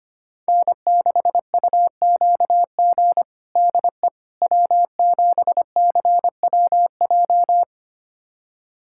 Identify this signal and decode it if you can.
Morse code, heard as a keyed tone, about 25 wpm, 700 Hz. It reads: N6UQG DE W7CWJ
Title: USA Callsign DE Callsign - 5 Characters - 25wpm